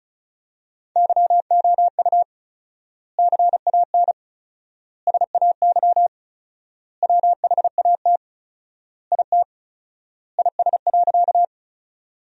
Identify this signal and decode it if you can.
Morse code, heard as a keyed tone, about 36 wpm, 700 Hz. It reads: YOU CAN SAY WHAT IT IS.
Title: 4x Word-Spacing - Sentences from Top 200 Words - 35wpm